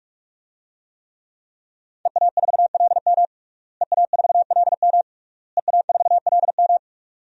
Morse code, heard as a keyed tone, about 45 wpm, 700 Hz. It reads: EA4LM EA4LM EA4LM
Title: DXCC-100 Call Signs - 5 Characters - Repeated 3x - 45wpm